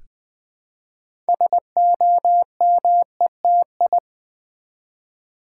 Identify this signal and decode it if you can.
Morse code, heard as a keyed tone, about 20 wpm, 700 Hz. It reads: SOMETI
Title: Build A Word - 3 Letters At A Time - 8-Letter Words - 20wpm